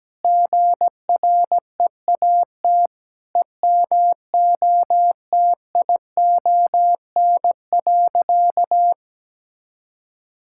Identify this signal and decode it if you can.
Morse code, heard as a keyed tone, about 17 wpm, 700 Hz. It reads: GREAT EMOTION.
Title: Sentences from Top 1600 Words 17wpm